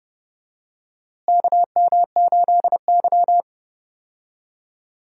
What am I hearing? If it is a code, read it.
Morse code, heard as a keyed tone, about 31 wpm, 700 Hz. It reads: KM8Y
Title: USA Call Signs - 4 Characters - Repeated 2x - 30wpm